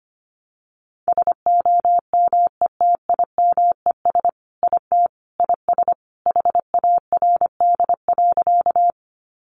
Morse code, heard as a keyed tone, about 25 wpm, 700 Hz. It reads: SOMETIMES IT IS HARD.
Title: Sentences from Top 600 Words 25wpm